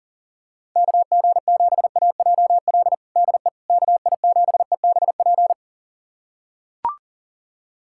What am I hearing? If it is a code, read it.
Morse code, heard as a keyed tone, about 40 wpm, 690 Hz. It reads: KG7AJL DE KI7EBP
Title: USA Callsign DE Callsign - 6 Characters - 40wpm